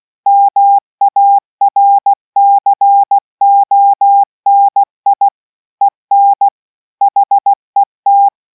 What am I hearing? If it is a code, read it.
Morse code, heard as a keyed tone, about 16 wpm, 800 Hz. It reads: MARCONI EN HET